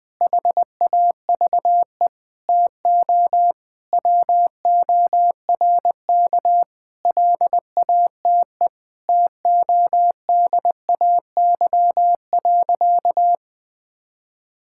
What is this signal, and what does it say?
Morse code, heard as a keyed tone, about 20 wpm, 700 Hz. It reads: HAVE TO WORK LATE TODAY.
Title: Sentences with 5-Word Phrases - 20wpm